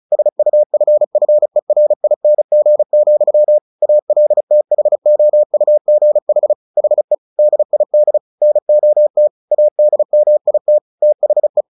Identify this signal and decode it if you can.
Morse code, heard as a keyed tone, about 34 wpm, 600 Hz. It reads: SUFFERING, ALTHOUGH HE DID NOT ADMIT THE
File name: the old man and the sea / the old man and the sea 1 35wpm